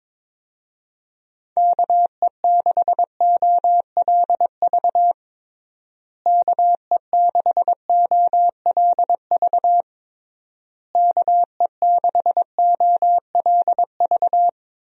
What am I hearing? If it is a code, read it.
Morse code, heard as a keyed tone, about 22 wpm, 700 Hz. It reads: KE6OLV KE6OLV KE6OLV
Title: USA Call Signs - 6 Characters - Repeated 3x - 22wpm